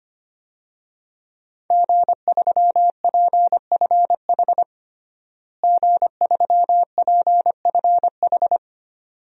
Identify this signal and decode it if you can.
Morse code, heard as a keyed tone, about 25 wpm, 700 Hz. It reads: G3PFH G3PFH
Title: DXCC-100 Call Signs - 5 Characters - Repeated 3x - 25wpm